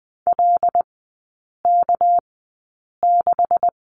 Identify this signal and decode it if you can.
Morse code, heard as a keyed tone, about 20 wpm, spaced as for 8 wpm, 700 Hz. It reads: LK6
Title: 205 Lesson 33e - Letter j - Three Characters 20wpm